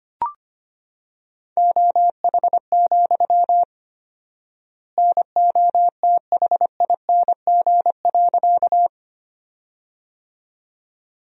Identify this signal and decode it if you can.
Morse code, heard as a keyed tone, about 25 wpm, 700 Hz. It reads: OH, NOTHING.
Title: 4x Word-Spacing - Sentences from Top 500 Words - 25wpm